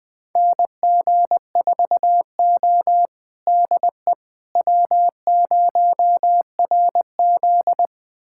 Morse code, heard as a keyed tone, about 20 wpm, 700 Hz. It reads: NG4O DE W0RZ
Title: USA Callsign DE Callsign - 4 Characters - 20wpm